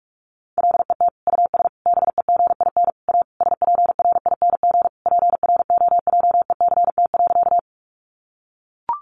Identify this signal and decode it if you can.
Morse code, heard as a keyed tone, about 44 wpm, 700 Hz. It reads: LET US BEGIN A SPRING PROJECT.
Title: Sentences from Top 1400 Words 45wpm